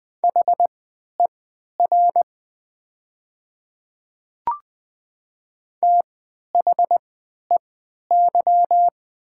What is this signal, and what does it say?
Morse code, heard as a keyed tone, about 20 wpm, spaced as for 11 wpm, 700 Hz. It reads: HER THEY
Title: 3x Character Spacing - Top 100 Words - 20wpm